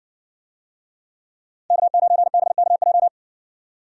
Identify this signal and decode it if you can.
Morse code, heard as a keyed tone, about 60 wpm, 700 Hz. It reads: K9BKJ